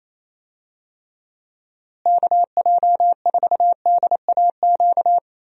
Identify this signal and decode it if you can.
Morse code, heard as a keyed tone, about 27 wpm, 700 Hz. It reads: KJ4DAQ
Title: USA Call Signs - 6 Characters - Repeated 3x - 28wpm